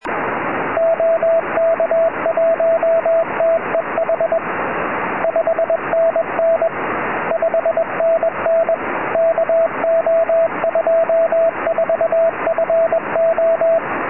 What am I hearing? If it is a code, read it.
Morse code, heard as a keyed tone, about 21 wpm, 660 Hz, with radio noise. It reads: OK1TEH 5NN 5NN KO24FO
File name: tropo / ly2wr 2m